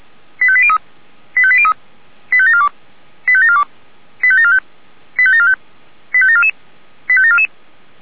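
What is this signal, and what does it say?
Five-tone selective call (ZVEI), heard as quick runs of five tones, tones of about 70 ms. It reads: 85803 85803 85742 85742 85745 85745 85744 85744
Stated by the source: ZVEI five-tone selective call 85726 received on SDR.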